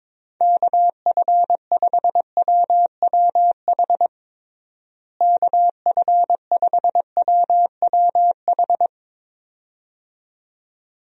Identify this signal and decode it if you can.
Morse code, heard as a keyed tone, about 22 wpm, 700 Hz. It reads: KF5WWH KF5WWH
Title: USA Call Signs - 6 Characters - Repeated 2x - 22wpm